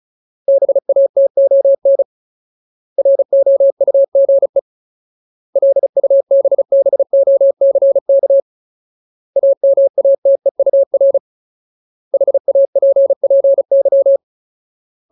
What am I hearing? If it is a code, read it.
Morse code, heard as a keyed tone, about 34 wpm, 550 Hz. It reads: BATON ROUGE LUBBOCK AMATEUR HAPPY
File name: Ultimate Word List / 35WPM Ultimate